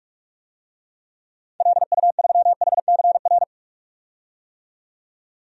Morse code, heard as a keyed tone, about 42 wpm, 700 Hz. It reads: LU3HCR
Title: DXCC-100 Call Signs - 6 Characters - 45wpm